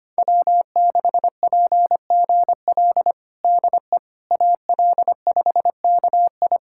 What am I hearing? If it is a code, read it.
Morse code, heard as a keyed tone, about 25 wpm, 700 Hz. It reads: W6PGL DE AL5KI
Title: USA Callsign DE Callsign - 5 Characters - 25wpm